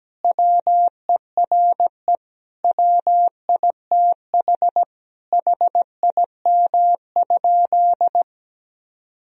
Morse code, heard as a keyed tone, about 17 wpm, 700 Hz. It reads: WERE WITH HIM?
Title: Sentences from Top 1000 Words - Review Part 1 - 17wpm